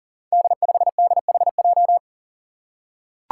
Morse code, heard as a keyed tone, about 40 wpm, 700 Hz. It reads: D5DHJ
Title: Five Challenging Letter-Number Combinations 40wpm